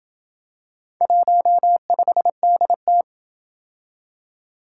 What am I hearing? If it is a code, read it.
Morse code, heard as a keyed tone, about 27 wpm, 700 Hz. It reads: 15DT